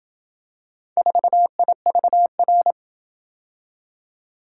Morse code, heard as a keyed tone, about 27 wpm, 700 Hz. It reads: 4IVR